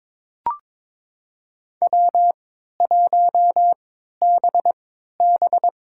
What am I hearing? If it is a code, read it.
Morse code, heard as a keyed tone, about 22 wpm, spaced as for 13 wpm, 700 Hz. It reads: W1BB
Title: USA Call Signs - 4 Characters - 3x Character Spacing - 22wpm